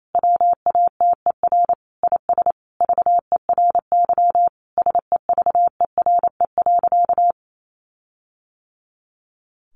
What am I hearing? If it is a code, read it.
Morse code, heard as a keyed tone, about 28 wpm, 700 Hz. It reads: WATER IS VERY SEVERE.
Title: Sentences from Top 1500 Words 28wpm